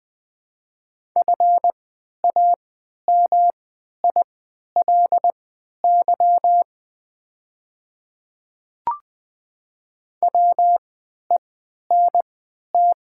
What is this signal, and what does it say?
Morse code, heard as a keyed tone, about 20 wpm, spaced as for 11 wpm, 700 Hz. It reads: FAMILY WENT
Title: 3x Character Spacing - Top 500 Words - 20wpm